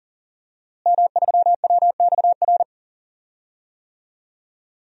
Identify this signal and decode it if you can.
Morse code, heard as a keyed tone, about 40 wpm, 700 Hz. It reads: M3WXR
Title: DXCC-100 Call Signs - 5 Characters - Repeated 3x - 40wpm